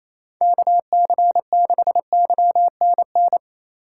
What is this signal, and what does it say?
Morse code, heard as a keyed tone, about 28 wpm, 700 Hz. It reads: KC6YNN
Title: USA Call Signs - 6 Characters - 2x Character Spacing - 28wpm